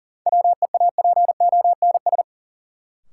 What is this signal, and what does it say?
Morse code, heard as a keyed tone, about 40 wpm, 700 Hz. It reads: WEAPONS
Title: Sets of 5 Words Encore - New Word Drill - 40wpm